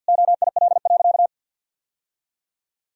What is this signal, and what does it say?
Morse code, heard as a keyed tone, about 50 wpm, 700 Hz. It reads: OIL.
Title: Sentences from Top 1000 Words - Review Part 4 - 50wpm